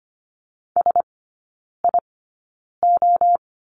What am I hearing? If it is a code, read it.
Morse code, heard as a keyed tone, about 25 wpm, spaced as for 9 wpm, 690 Hz. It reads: SIO
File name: Lesson 14e - Number 2 - Three Characters 25wpm